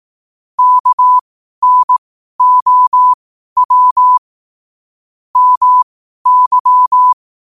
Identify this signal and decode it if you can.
Morse code, heard as a keyed tone, about 18 wpm, spaced as for 12 wpm, 1000 Hz. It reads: KNOW MY
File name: qotdfiles / 12wpm